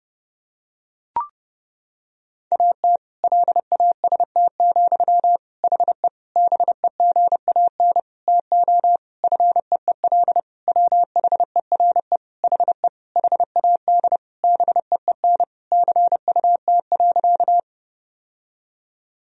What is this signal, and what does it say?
Morse code, heard as a keyed tone, about 32 wpm, 700 Hz. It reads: AT LAST, HE BEGAN TO FEEL WHERE HE HAD BEEN CUT.
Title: Sentences from Top 1000 Words - Review Part 1 - 30wpm